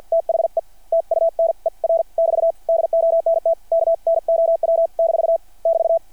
Morse code, heard as a keyed tone, about 53 wpm, spaced as for 45 wpm, 660 Hz. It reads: THE TUNER <BT> DONT KNOW - <BT>
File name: archived / CRSnet